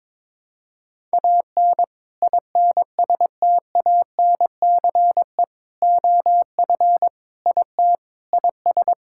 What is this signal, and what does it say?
Morse code, heard as a keyed tone, about 22 wpm, 700 Hz. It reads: AN INSTANCE OF IT IS
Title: Sentences from Top 1000 Words - Review Part 5 - 22wpm